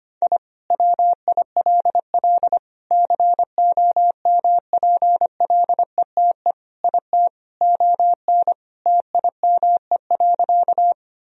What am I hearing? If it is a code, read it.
Morse code, heard as a keyed tone, about 25 wpm, 700 Hz. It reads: I WILL COMPLETE IT ON TIME.
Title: Sentences from Top 1000 Words - Review Part 2 - 25wpm